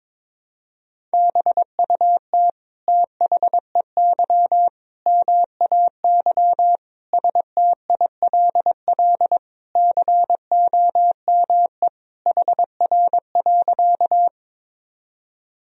Morse code, heard as a keyed tone, about 22 wpm, 700 Hz. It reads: BUT THEY MAY STILL COME HR.
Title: Sentences from Top 300 Words and Abbreviations 22wpm